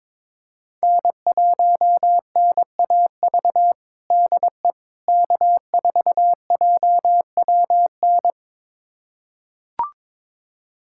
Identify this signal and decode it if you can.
Morse code, heard as a keyed tone, about 22 wpm, 700 Hz. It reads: N1NAV DE K4JWN
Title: USA Callsign DE Callsign - 5 Characters - 22wpm